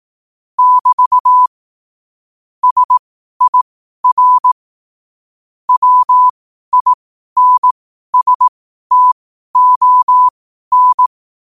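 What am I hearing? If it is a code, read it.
Morse code, heard as a keyed tone, about 18 wpm, spaced as for 12 wpm, 1000 Hz. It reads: <BT> SIR WINSTON
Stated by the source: Morse code at 12 wpm.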